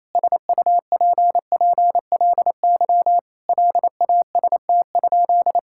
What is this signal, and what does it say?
Morse code, heard as a keyed tone, about 28 wpm, 700 Hz. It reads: SUPPLY LAST?
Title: Sentences from Top 1000 Words - Review Part 5 - 28wpm